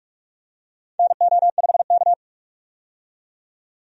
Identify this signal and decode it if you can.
Morse code, heard as a keyed tone, about 44 wpm, 700 Hz. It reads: NO5K
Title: USA Call Signs - 4 Characters - 45wpm